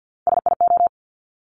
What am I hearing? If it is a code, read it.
Morse code, heard as a keyed tone, about 50 wpm, 700 Hz. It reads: SIO
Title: Top Three Letter Combinations in Words 50wpm